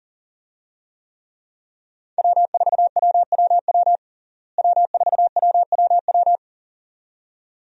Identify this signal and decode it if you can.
Morse code, heard as a keyed tone, about 40 wpm, 700 Hz. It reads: W4WWW W4WWW
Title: USA Call Signs - 5 Characters - Repeated 2x - 40wpm